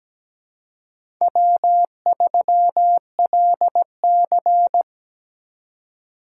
Morse code, heard as a keyed tone, about 17 wpm, 700 Hz. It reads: W3LC
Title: USA Call Signs - 4x Character Spacing - 17wpm